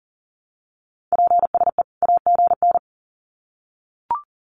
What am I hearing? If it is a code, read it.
Morse code, heard as a keyed tone, about 40 wpm, 700 Hz. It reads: PSE AGN
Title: Common QSO Phrases 40wpm